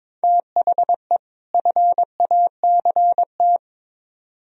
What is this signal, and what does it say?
Morse code, heard as a keyed tone, about 26 wpm, 700 Hz. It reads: THE FACT
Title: Speed-Racing - Sets of 2 Words 22wpm